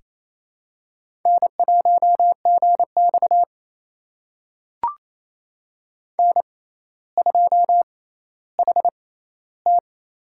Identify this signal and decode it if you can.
Morse code, heard as a keyed tone, about 27 wpm, spaced as for 14 wpm, 700 Hz. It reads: N1GX N2HT
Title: USA Call Signs - 4 Characters - 6x Character Spacing - 28wpm